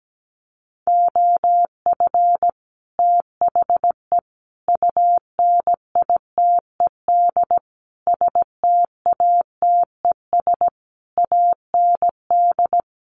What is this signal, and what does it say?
Morse code, heard as a keyed tone, about 17 wpm, 700 Hz. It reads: OF THE UNITED STATES AND
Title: Sets of 5 Words 17wpm